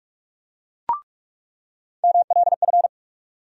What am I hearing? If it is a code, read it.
Morse code, heard as a keyed tone, about 46 wpm, 700 Hz. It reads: MLF
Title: USA Call Sign Postfixes - 3 Letters 45wpm